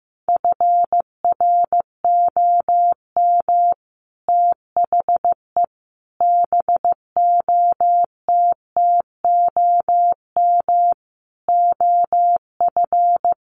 Morse code, heard as a keyed tone, about 15 wpm, 700 Hz. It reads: FROM THE BOTTOM OF
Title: Sets of 5 Words Encore2 15wpm